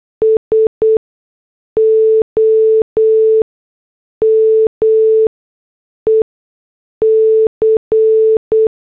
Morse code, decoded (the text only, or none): SOME C